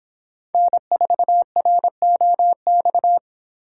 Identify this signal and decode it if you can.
Morse code, heard as a keyed tone, about 24 wpm, 700 Hz. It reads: N4ROX